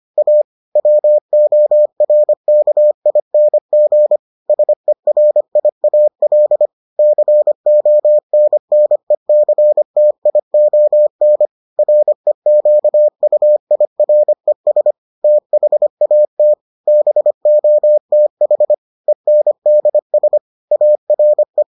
Morse code, heard as a keyed tone, about 25 wpm, 600 Hz. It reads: A WORKING SERIAL CONNECTION REQUIRES THAT BOTH ENDS ARE